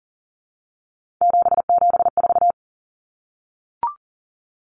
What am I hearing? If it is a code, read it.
Morse code, heard as a keyed tone, about 40 wpm, 700 Hz. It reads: 774
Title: Common Cut Numbers - 3-Digit 40wpm